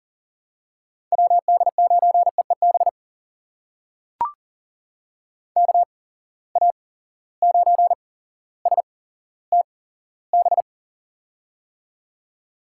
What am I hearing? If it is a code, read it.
Morse code, heard as a keyed tone, about 40 wpm, spaced as for 17 wpm, 700 Hz. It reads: WD9EEB KA9STB
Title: USA Call Signs - 6 Characters - 8x Character Spacing - 40wpm